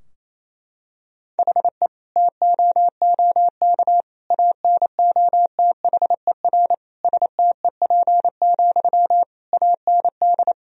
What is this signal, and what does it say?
Morse code, heard as a keyed tone, about 27 wpm, 700 Hz. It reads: HE TOOK ANOTHER STEP, AND
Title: Sentences from Top 1000 Words - Review Part 2 - 28wpm